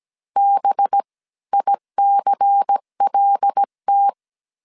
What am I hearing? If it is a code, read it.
Morse code, heard as a keyed tone, about 17 wpm, spaced as for 14 wpm, 800 Hz. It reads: BICLT
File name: CW / DET / 1 6 1500C S C 16wpm 2733